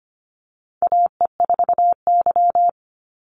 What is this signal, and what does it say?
Morse code, heard as a keyed tone, about 25 wpm, 700 Hz. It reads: AE4Y
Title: NA SOTA Chaser Callsigns 25wpm